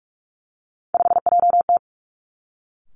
Morse code, heard as a keyed tone, about 46 wpm, 700 Hz. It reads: 5JT
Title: Three Challenging Letter-Number Combinations 45wpm